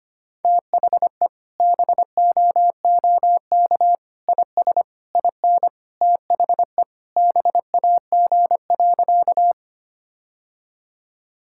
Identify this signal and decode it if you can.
Morse code, heard as a keyed tone, about 25 wpm, 700 Hz. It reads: THE BOOK IS IN THE BAG.
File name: practice / 25wpm / Sentences with 3-Word Phrases - 25wpm